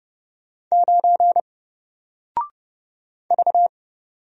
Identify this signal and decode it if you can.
Morse code, heard as a keyed tone, about 30 wpm, 700 Hz. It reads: 9 V
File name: Single-Letter-Number-30wpm